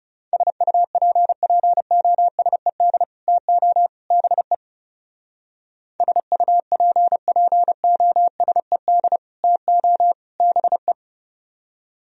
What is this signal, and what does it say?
Morse code, heard as a keyed tone, about 39 wpm, 700 Hz. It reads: SUPPOSED TO BE SUPPOSED TO BE